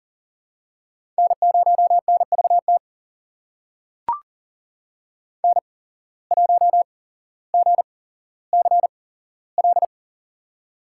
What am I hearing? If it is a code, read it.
Morse code, heard as a keyed tone, about 42 wpm, spaced as for 17 wpm, 700 Hz. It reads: N0NVT N1GCL
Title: USA Call Signs - 8x Character Spacing - 40wpm